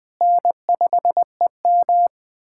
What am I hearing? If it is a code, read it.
Morse code, heard as a keyed tone, about 20 wpm, 700 Hz. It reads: N5EM
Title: USA Callsign DE Callsign - 4 Characters - 20wpm